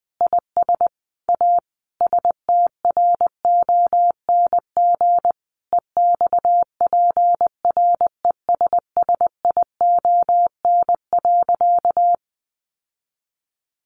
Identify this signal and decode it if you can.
Morse code, heard as a keyed tone, about 20 wpm, 700 Hz. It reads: IS A STRONG EXPRESSION.
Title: Sentences from Top 1200 Words 20wpm